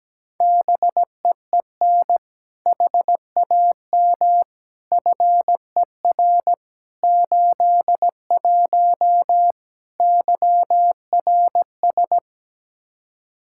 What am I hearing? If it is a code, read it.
Morse code, heard as a keyed tone, about 17 wpm, 700 Hz. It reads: BEEN HAM FER 81 YRS
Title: Practice QSO - Exchange 3 - More Info - 17wpm